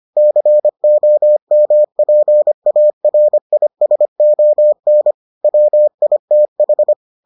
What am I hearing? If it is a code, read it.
Morse code, heard as a keyed tone, about 25 wpm, 600 Hz. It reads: COMPARISON WITH